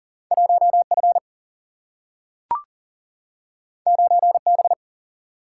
Morse code, heard as a keyed tone, about 39 wpm, 700 Hz. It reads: 1 F 9 B
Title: Single-Letter Single-Number 40wpm